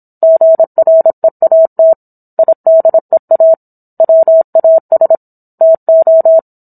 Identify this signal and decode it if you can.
Morse code, heard as a keyed tone, about 27 wpm, 650 Hz. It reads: GREAT IDEA WAS TO